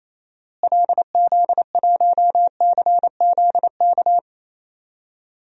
Morse code, heard as a keyed tone, about 28 wpm, 700 Hz. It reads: LZ1CZK
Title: DXCC-100 Call Signs - 6 Characters - Repeated 3x - 28wpm